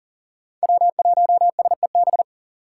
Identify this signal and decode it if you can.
Morse code, heard as a keyed tone, about 52 wpm, 700 Hz. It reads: W1SEB